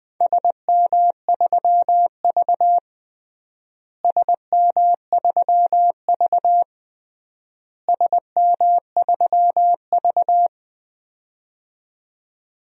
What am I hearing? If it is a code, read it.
Morse code, heard as a keyed tone, about 20 wpm, 700 Hz. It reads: SM3V SM3V SM3V
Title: DXCC-100 Call Signs - 4 Characters - Repeated 3x - 20wpm